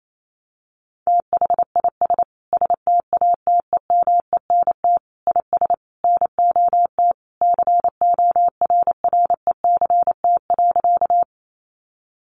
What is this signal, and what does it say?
Morse code, heard as a keyed tone, about 28 wpm, 700 Hz. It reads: THIS STATEMENT IS NOT CORRECT.
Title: Sentences from Top 1200 Words 28wpm